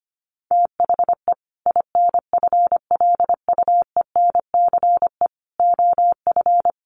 Morse code, heard as a keyed tone, about 25 wpm, 700 Hz. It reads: THE INFLUENCE OF
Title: Sentences from Top 1100 Words 25wpm